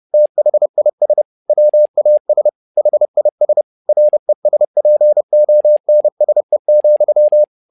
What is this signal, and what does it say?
Morse code, heard as a keyed tone, about 30 wpm, 600 Hz. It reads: THIS WAS HIS RESPONSE,
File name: learning center / tower trouble / tower trouble 30wpm